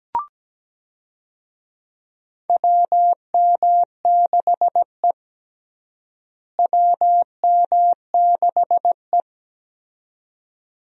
Morse code, heard as a keyed tone, about 17 wpm, 700 Hz. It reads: WM6E WM6E
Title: USA Call Signs - 4 Characters - Repeated 2x - 17wpm